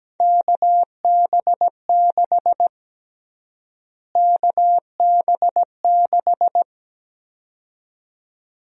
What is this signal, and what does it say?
Morse code, heard as a keyed tone, about 17 wpm, 700 Hz. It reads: KB6 KB6
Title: USA Call Sign Prefixes - Repeated 2x - 17wpm